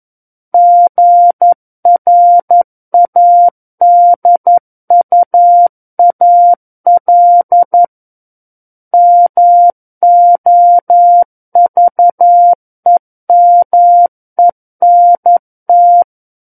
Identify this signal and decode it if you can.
Morse code, heard as a keyed tone, about 11 wpm, 700 Hz. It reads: GRADUAL MOVEMENT
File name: 11 WPM